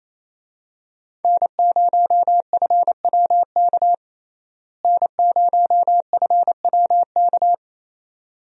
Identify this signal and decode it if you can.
Morse code, heard as a keyed tone, about 28 wpm, 700 Hz. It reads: N0FWK N0FWK
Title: USA Call Signs - 5 Characters - Repeated 2x - 28wpm